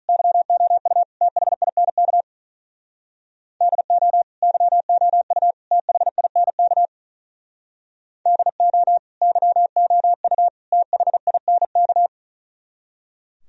Morse code, heard as a keyed tone, about 45 wpm, 700 Hz. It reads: YOU THINK DO YOU THINK DO YOU THINK